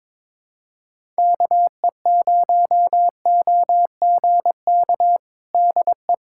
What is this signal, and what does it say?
Morse code, heard as a keyed tone, about 22 wpm, 700 Hz. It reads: KE0OGK DE
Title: USA Callsign DE Callsign - 6 Characters - 22wpm